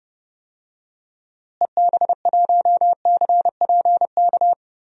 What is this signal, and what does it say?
Morse code, heard as a keyed tone, about 30 wpm, 700 Hz. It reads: EB1CPK
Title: DXCC-100 Call Signs - 6 Characters - Repeated 3x - 30wpm